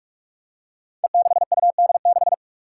Morse code, heard as a keyed tone, about 41 wpm, 700 Hz. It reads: E6UD6
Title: Five Challenging Letter-Number Combinations 45wpm